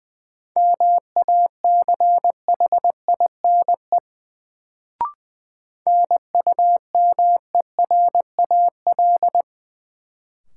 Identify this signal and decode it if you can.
Morse code, heard as a keyed tone, about 20 wpm, 700 Hz. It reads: MACHINE NUMERAL
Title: Sentences from Top 1000 Words - Review Part 2 - New Word Drill - 20wpm